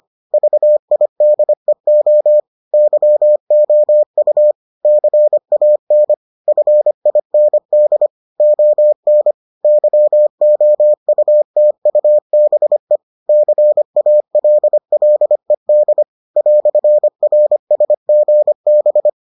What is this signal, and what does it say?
Morse code, heard as a keyed tone, about 25 wpm, 600 Hz. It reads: VIDEO YOU CAN FIND ON YOUTUBE CALLED "RSGB